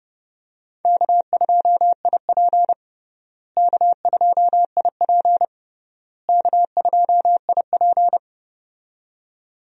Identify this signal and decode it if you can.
Morse code, heard as a keyed tone, about 30 wpm, 700 Hz. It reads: K2IP K2IP K2IP
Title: USA Call Signs - 4 Characters - Repeated 3x - 30wpm